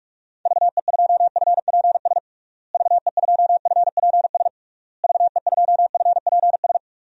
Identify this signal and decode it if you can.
Morse code, heard as a keyed tone, about 46 wpm, 700 Hz. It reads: VE2FPS VE2FPS VE2FPS
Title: DXCC-100 Call Signs - 6 Characters - Repeated 3x - 45wpm